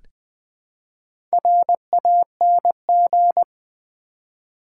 Morse code, heard as a keyed tone, about 20 wpm, 700 Hz. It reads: RANG
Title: Build A Word - 1 Letter At A Time - 5-Letter Words - 20wpm